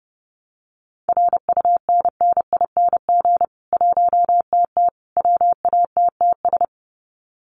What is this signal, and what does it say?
Morse code, heard as a keyed tone, about 30 wpm, 700 Hz. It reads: RUNNING 1TT WATTS
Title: Common QSO Phrases 30wpm